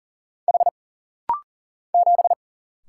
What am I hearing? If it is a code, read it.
Morse code, heard as a keyed tone, about 40 wpm, 700 Hz. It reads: H 7
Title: Single Letter-Number Alt - Rapid-Fire 40wpm